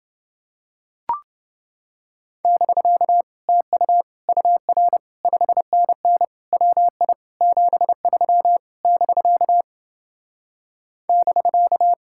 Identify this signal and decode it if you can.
Morse code, heard as a keyed tone, about 30 wpm, 700 Hz. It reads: <BK> TU UR 5NN WI 73 <BK> <BK>